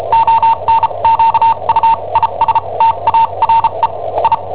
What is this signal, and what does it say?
Morse code, heard as a keyed tone, about 34 wpm, spaced as for 27 wpm, 910 Hz. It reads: ONQUISTARE I
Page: Cassetta esercizi preparata da I7OHP per IW5ELP